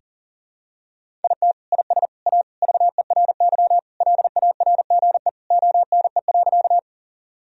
Morse code, heard as a keyed tone, about 40 wpm, 700 Hz. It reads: IT IS A VERY LARGE ONE.
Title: Sentences from Top 1000 Words - Review Part 1 - 40wpm